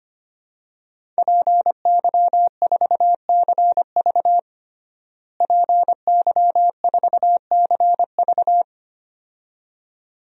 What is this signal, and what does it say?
Morse code, heard as a keyed tone, about 25 wpm, 700 Hz. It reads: PY4CV PY4CV
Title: DXCC-100 Call Signs - 5 Characters - Repeated 2x - 25wpm